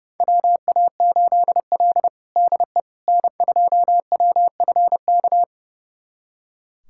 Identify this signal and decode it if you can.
Morse code, heard as a keyed tone, about 30 wpm, 700 Hz. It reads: WA8L DE N2WFK
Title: USA Callsign DE Callsign 30wpm